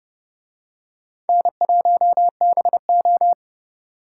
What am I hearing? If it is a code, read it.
Morse code, heard as a keyed tone, about 29 wpm, 700 Hz. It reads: N1BO